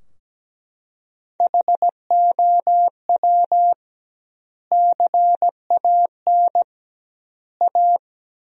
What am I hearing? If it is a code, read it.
Morse code, heard as a keyed tone, about 17 wpm, 700 Hz. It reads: HOW CAN A
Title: 2x Word-Spacing - Sentences from Top 500 Words - 17wpm